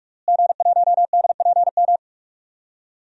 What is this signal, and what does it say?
Morse code, heard as a keyed tone, about 46 wpm, 700 Hz. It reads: G1DPM
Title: DXCC-100 Call Signs - 5 Characters - Repeated 3x - 45wpm